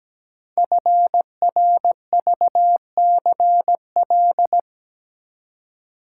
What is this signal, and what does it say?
Morse code, heard as a keyed tone, about 17 wpm, 700 Hz. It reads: FRVCL
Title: Code Groups - 5 Letters - 17wpm